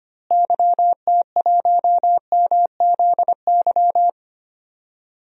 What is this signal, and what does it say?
Morse code, heard as a keyed tone, about 25 wpm, 700 Hz. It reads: YT1MZY
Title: DXCC-100 Call Signs - 6 Characters - Repeated 2x - 25wpm